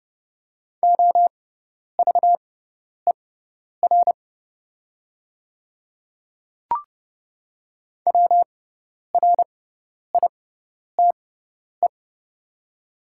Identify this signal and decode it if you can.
Morse code, heard as a keyed tone, about 30 wpm, spaced as for 10 wpm, 700 Hz. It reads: OVER WRITE
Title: 6x Character Spacing - Top 100 Words - 30wpm